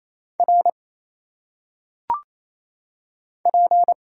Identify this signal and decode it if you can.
Morse code, heard as a keyed tone, about 27 wpm, 700 Hz. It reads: R P
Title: Single Letters - Phonetic Pronunciation 28wpm